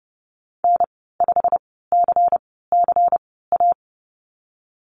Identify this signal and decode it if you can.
Morse code, heard as a keyed tone, about 30 wpm, spaced as for 17 wpm, 700 Hz. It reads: N5CCA
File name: Lesson 37g - Letter x - Call Signs 30wpm